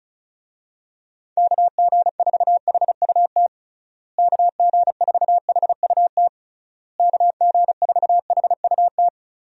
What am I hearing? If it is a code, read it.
Morse code, heard as a keyed tone, about 34 wpm, 700 Hz. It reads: KG4HUT KG4HUT KG4HUT
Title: USA Call Signs - 6 Characters - Repeated 3x - 35wpm